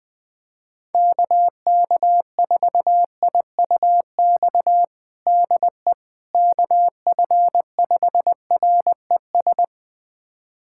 Morse code, heard as a keyed tone, about 20 wpm, 700 Hz. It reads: KK4IUX DE KF5RES
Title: USA Callsign DE Callsign 20wpm